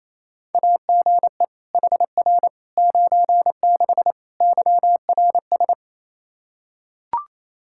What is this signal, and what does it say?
Morse code, heard as a keyed tone, about 28 wpm, 700 Hz. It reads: AGE HR 96 YRS
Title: Practice QSO - Exchange 3 - More Info - 28wpm